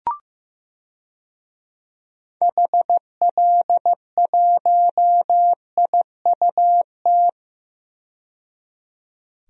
Morse code, heard as a keyed tone, about 15 wpm, 700 Hz. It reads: HL1IUT
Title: DXCC-100 Call Signs - 6 Characters - 15wpm